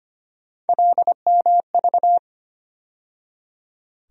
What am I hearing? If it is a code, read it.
Morse code, heard as a keyed tone, about 25 wpm, 700 Hz. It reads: LMV
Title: USA Call Sign Postfixes - 3 Letters 25wpm